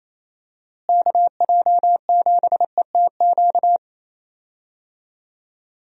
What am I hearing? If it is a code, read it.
Morse code, heard as a keyed tone, about 27 wpm, 700 Hz. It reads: KJ7ETQ
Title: USA Call Signs - 6 Characters - Repeated 2x - 28wpm